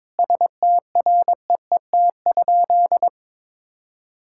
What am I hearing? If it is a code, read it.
Morse code, heard as a keyed tone, about 22 wpm, 700 Hz. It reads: STREET?
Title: Sentences from Top 1000 Words - Review Part 6 - 22wpm